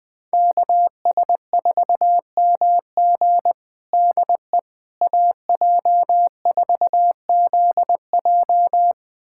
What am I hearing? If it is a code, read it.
Morse code, heard as a keyed tone, about 20 wpm, 700 Hz. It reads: KS4MG DE AJ4ZJ
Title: USA Callsign DE Callsign - 5 Characters - 20wpm